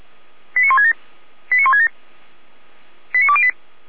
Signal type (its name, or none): five-tone selective call